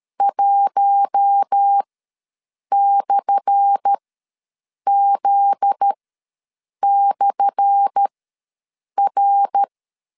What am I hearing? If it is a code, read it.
Morse code, heard as a keyed tone, about 13 wpm, spaced as for 7 wpm, 800 Hz. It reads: 1/Z/R
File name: LEZIONI / 10 SIGNS / 10A SLASH ws 12wpm 3443